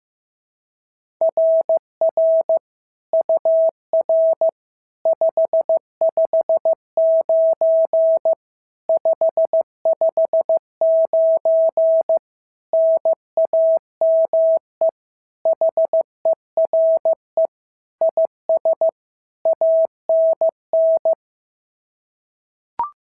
Morse code, heard as a keyed tone, about 14 wpm, 650 Hz. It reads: RR UR 559 559 NAME HERE IS ANN
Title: SOTA QSO Practice 15wpm